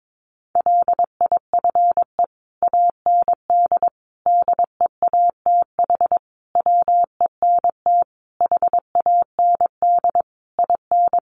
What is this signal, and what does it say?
Morse code, heard as a keyed tone, about 22 wpm, 700 Hz. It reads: LIFE AND DEATH WENT HAND IN
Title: Sentences from Top 1100 Words 22wpm